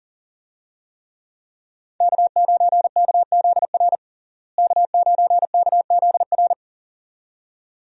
Morse code, heard as a keyed tone, about 40 wpm, 700 Hz. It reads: K9KZR K9KZR
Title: USA Call Signs - 5 Characters - Repeated 2x - 40wpm